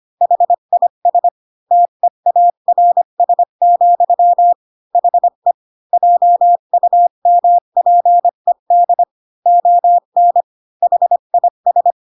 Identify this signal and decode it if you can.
Morse code, heard as a keyed tone, about 25 wpm, 700 Hz. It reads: HIS TEARS, HE JUMPED ON HIS
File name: Pinocchio / Pinocchio ch8 20 25 ik1hge